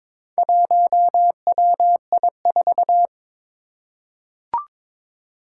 Morse code, heard as a keyed tone, about 22 wpm, 700 Hz. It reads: 1WI4
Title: Four Challenging Letter-Number Combinations 22wpm